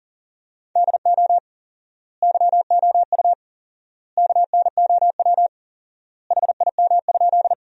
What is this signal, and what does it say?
Morse code, heard as a keyed tone, about 37 wpm, 700 Hz. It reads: DO YOU KNOW HIM?
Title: 4x Word-Spacing - Sentences from Top 100 Words - 40wpm